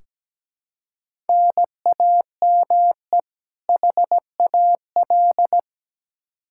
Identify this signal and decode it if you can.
Morse code, heard as a keyed tone, about 17 wpm, 700 Hz. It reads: NAME HAL
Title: Practice QSO - Exchange 1 - Beginning - 17wpm